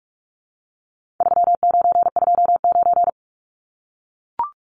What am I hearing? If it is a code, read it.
Morse code, heard as a keyed tone, about 44 wpm, 700 Hz. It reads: 3929
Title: Numbers - 4-Digit 45wpm